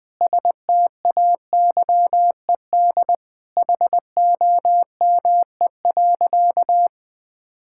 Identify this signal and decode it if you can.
Morse code, heard as a keyed tone, about 20 wpm, 700 Hz. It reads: STAYED HOME.
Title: Sentences with 4-Word Phrases - 20wpm